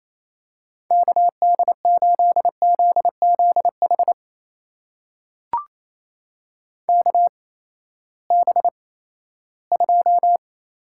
Morse code, heard as a keyed tone, about 28 wpm, spaced as for 16 wpm, 700 Hz. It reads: KD8ZZH KB2
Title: USA Call Signs - 6 Characters - 8x Character Spacing - 28wpm